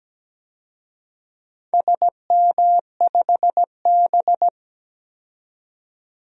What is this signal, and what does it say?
Morse code, heard as a keyed tone, about 17 wpm, 700 Hz. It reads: SM5B
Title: DXCC-100 Call Signs - 4 Characters - 17wpm